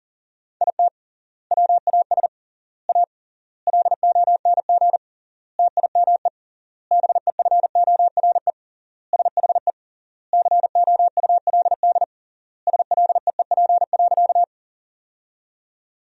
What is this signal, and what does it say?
Morse code, heard as a keyed tone, about 40 wpm, 700 Hz. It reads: IT WAS A LONG TIME BEFORE SHE COULD SLEEP.
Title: 3x Word-Spacing - Sentences from Top 500 Words - 40wpm